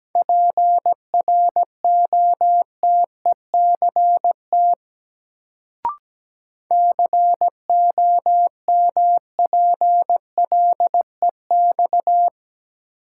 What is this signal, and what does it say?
Morse code, heard as a keyed tone, about 17 wpm, 700 Hz. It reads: PROTECT COMPLEX
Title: Sentences from Top 900 Words - New Word Drill - 17wpm